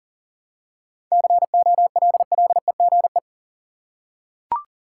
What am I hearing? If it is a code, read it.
Morse code, heard as a keyed tone, about 40 wpm, 700 Hz. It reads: COLLEGE
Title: Seven Letter Words - 40wpm